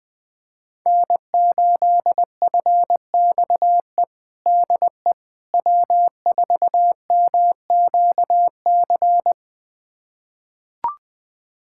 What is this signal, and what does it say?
Morse code, heard as a keyed tone, about 20 wpm, 700 Hz. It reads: N8FXE DE W4MQC
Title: USA Callsign DE Callsign 20wpm